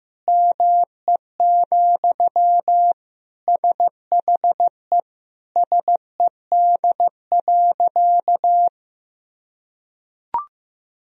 Morse code, heard as a keyed tone, about 15 wpm, 700 Hz. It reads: ME, SHE SED.
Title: Sentences from Top 500 Words and Abbreviations 15wpm